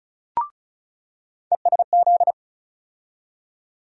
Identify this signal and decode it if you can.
Morse code, heard as a keyed tone, about 36 wpm, 700 Hz. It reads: ESZ
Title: USA Call Sign Postfixes - 3 Letters 35wpm